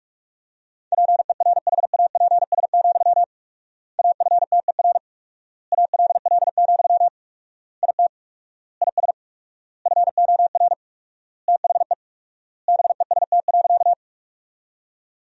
Morse code, heard as a keyed tone, about 47 wpm, 690 Hz. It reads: PERHAPS, AFTER ALL, IT IS FOR THE BEST.
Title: 4x Word-Spacing - Sentences from Top 500 Words - 45wpm